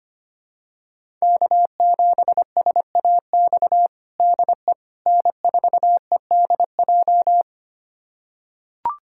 Morse code, heard as a keyed tone, about 25 wpm, 700 Hz. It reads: K7SAX DE N4EDJ
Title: USA Callsign DE Callsign - 5 Characters - 25wpm